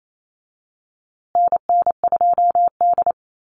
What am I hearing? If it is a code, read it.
Morse code, heard as a keyed tone, about 28 wpm, 700 Hz. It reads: NN2D
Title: USA Call Signs Repeated 2x 28wpm